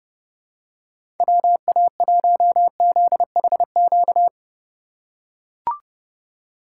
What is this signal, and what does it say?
Morse code, heard as a keyed tone, about 30 wpm, 700 Hz. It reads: WA1ZHQ
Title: USA Call Signs - 6 Characters - 3x Character Spacing - 30wpm